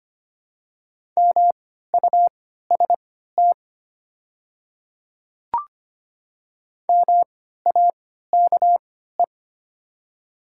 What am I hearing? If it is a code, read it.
Morse code, heard as a keyed tone, about 25 wpm, spaced as for 14 wpm, 690 Hz. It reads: MUST MAKE
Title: 3x Character Spacing - Top 200 Words - 25wpm